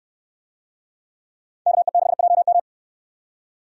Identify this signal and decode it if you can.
Morse code, heard as a keyed tone, about 67 wpm, 700 Hz. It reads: KE6JM